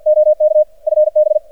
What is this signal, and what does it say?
Morse code, heard as a keyed tone, about 53 wpm, 610 Hz, with radio noise. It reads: OK FB
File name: archived / CRSnet